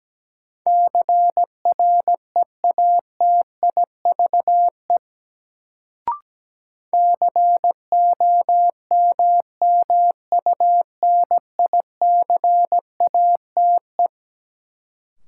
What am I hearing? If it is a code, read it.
Morse code, heard as a keyed tone, about 17 wpm, 700 Hz. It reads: CREATIVE COMMUNICATE
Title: Sentences from Top 1300 Words - New Word Drill Alt - 17wpm